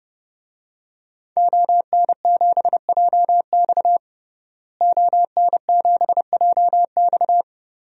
Morse code, heard as a keyed tone, about 32 wpm, 700 Hz. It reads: ON7JX ON7JX
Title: DXCC-100 Call Signs - 5 Characters - Repeated 3x - 30wpm